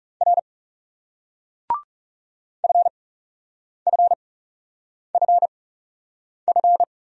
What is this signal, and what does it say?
Morse code, heard as a keyed tone, about 37 wpm, spaced as for 8 wpm, 700 Hz. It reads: R FFFF
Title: Speed-Racing Single Letters 30wpm